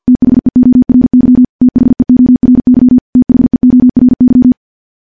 Morse code, heard as a keyed tone, about 50 wpm, 250 Hz. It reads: THEORY THEORY THEORY
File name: morse / morse-training-50WPM